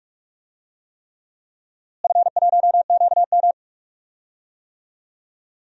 Morse code, heard as a keyed tone, about 45 wpm, 700 Hz. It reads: F1QM
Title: DXCC-100 Call Signs - 4 Characters - Repeated 3x - 45wpm